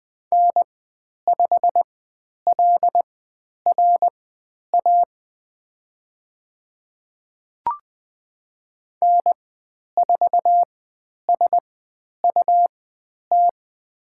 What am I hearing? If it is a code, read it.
Morse code, heard as a keyed tone, about 20 wpm, spaced as for 10 wpm, 700 Hz. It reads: N5LRA N4SUT
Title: 102 Lesson 17g - Letter u - Call Signs 20wpm